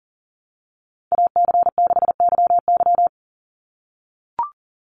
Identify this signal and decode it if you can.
Morse code, heard as a keyed tone, about 37 wpm, 700 Hz. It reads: AC6YY
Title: Call Signs - 40wpm